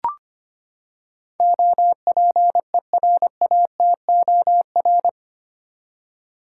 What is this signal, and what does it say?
Morse code, heard as a keyed tone, about 25 wpm, 700 Hz. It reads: OPERATOR
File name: Eight Letter Words - 25wpm